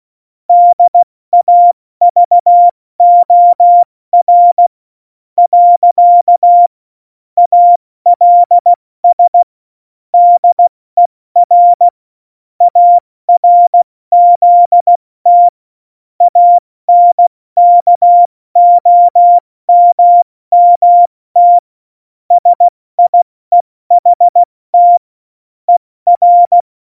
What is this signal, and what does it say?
Morse code, heard as a keyed tone, about 16 wpm, spaced as for 14 wpm, 700 Hz. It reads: DAVOR. ALS DER ARZT ANKOMMT SIEHT ER
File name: Morsekurs / Witze / Witz 16. Januar 2018, 70 F80